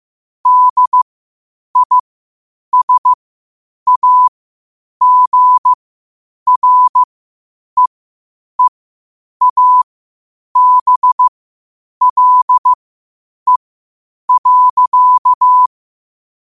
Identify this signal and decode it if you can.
Morse code, heard as a keyed tone, about 15 wpm, spaced as for 8 wpm, 1000 Hz. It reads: DISAGREEABLE.
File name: qotdfiles / 8wpm